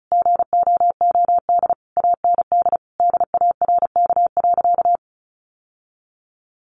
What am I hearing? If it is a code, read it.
Morse code, heard as a keyed tone, about 34 wpm, 700 Hz. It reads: GOOD AND DARK.
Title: Sentences from Top 600 Words 35wpm